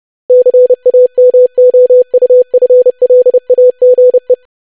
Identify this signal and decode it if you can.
Morse code, heard as a keyed tone, about 30 wpm, 500 Hz. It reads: CAMOUFLAGE
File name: lesson / web / longwords-25wpm